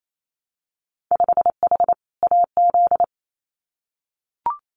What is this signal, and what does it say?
Morse code, heard as a keyed tone, about 28 wpm, 700 Hz. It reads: 5H AZ
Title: Winter Field Day Exchanges Encore 28wpm